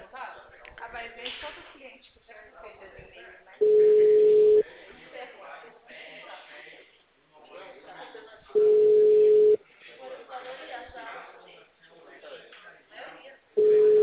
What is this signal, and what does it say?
Ringback tone (Europe, 425 Hz), with speech.
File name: out- -2005- - - .36011